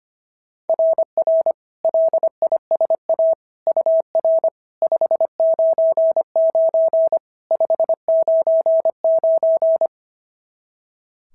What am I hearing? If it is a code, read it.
Morse code, heard as a keyed tone, about 23 wpm, 650 Hz. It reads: RR LISA UR 599 599
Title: SOTA QSO Practice 25wpm